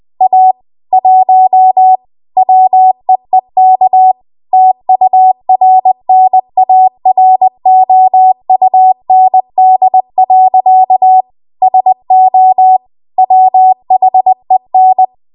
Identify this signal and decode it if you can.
Morse code, heard as a keyed tone, about 20 wpm, 750 Hz. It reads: A 1 WEEK TURNAROUND. SO WHEN
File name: Archive / 20 WPM / 20WPM